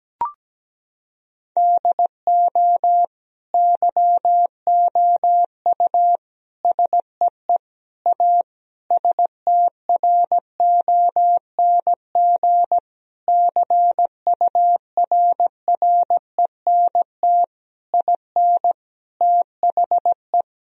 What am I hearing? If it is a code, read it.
Morse code, heard as a keyed tone, about 17 wpm, 700 Hz. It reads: DO YOU SEE A STRONG CURRENT IN THE
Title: Sentences from Top 1000 Words - Review Part 4 - 17wpm